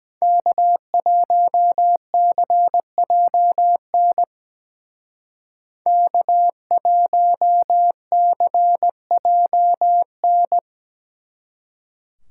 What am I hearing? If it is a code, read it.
Morse code, heard as a keyed tone, about 22 wpm, 700 Hz. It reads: K1CJN K1CJN